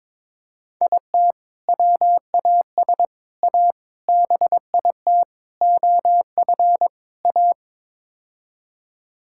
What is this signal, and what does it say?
Morse code, heard as a keyed tone, about 22 wpm, 700 Hz. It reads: IT WAS A BIT OF A
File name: practice / 22wpm / Sets of 6 Words - 22wpm